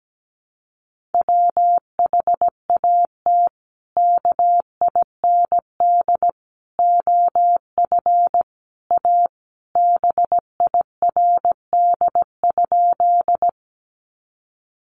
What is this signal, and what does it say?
Morse code, heard as a keyed tone, about 17 wpm, 700 Hz. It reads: WHAT KIND OF A BIRD?
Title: Sentences from Top 400 Words 17wpm